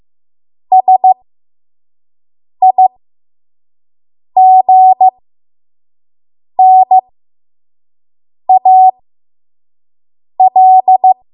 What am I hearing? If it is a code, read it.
Morse code, heard as a keyed tone, about 15 wpm, spaced as for 5 wpm, 750 Hz. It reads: SIGNAL
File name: Morse / 05WPM